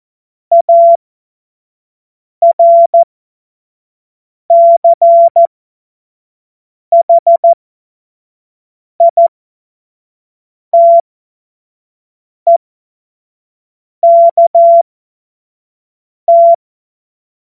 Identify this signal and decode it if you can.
Morse code, heard as a keyed tone, about 14 wpm, spaced as for 5 wpm, 680 Hz. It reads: ARCHITEKT
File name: Morsekurs / Witze / Witz 27. Januar 2026 25 F70